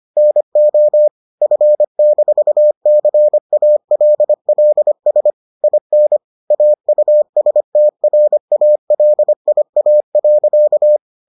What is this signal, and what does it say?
Morse code, heard as a keyed tone, about 25 wpm, 600 Hz. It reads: NO F-CALLS IN AUSTRALIA.